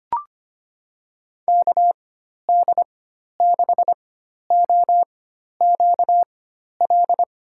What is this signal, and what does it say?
Morse code, heard as a keyed tone, about 25 wpm, spaced as for 12 wpm, 700 Hz. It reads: KD6OQL
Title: USA Call Signs - 4x Character Spacing - 25wpm